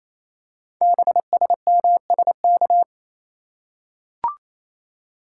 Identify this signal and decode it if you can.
Morse code, heard as a keyed tone, about 29 wpm, 690 Hz. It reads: BSMSK
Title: Code Groups - 5 Letters - 28wpm